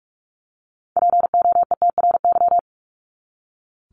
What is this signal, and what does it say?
Morse code, heard as a keyed tone, about 44 wpm, 700 Hz. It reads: POETRY
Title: Top 1751-2000 Words 45wpm